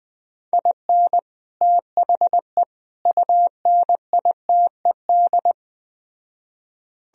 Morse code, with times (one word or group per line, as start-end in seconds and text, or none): 0.5-1.2 IN
1.6-2.6 THE
3.1-5.5 UNITED